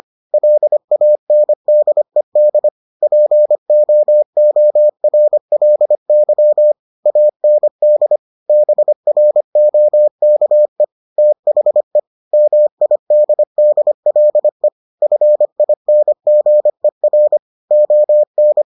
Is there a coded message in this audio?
Yes: Morse code from 0.3 to 18.6 s, heard as a keyed tone, about 25 wpm, 600 Hz. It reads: LANDED POORLY AND BROKE THE MIDDLE FINGER ON